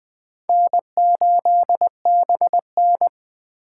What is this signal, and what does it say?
Morse code, heard as a keyed tone, about 24 wpm, 700 Hz. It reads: N8BN
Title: Speed Racing - USA Call Signs 20wpm